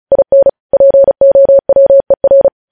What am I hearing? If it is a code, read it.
Morse code, heard as a keyed tone, about 36 wpm, 550 Hz. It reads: IN POWER
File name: podcast / News headlines-1x-35x35